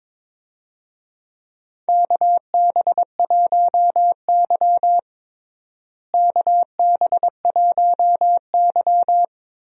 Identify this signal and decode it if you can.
Morse code, heard as a keyed tone, about 22 wpm, 700 Hz. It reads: KB1Y KB1Y
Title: USA Call Signs - 4 Characters - Repeated 3x - 22wpm